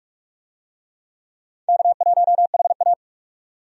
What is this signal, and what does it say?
Morse code, heard as a keyed tone, about 46 wpm, 700 Hz. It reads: K1HA
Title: USA Call Signs - 4 Characters - 45wpm